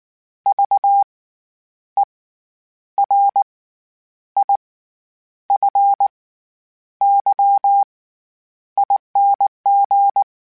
Morse code, heard as a keyed tone, about 19 wpm, spaced as for 9 wpm, 800 Hz. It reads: VERIFYING
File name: Special INGs / Special INGs-mp3s / SlowToFaster ing @18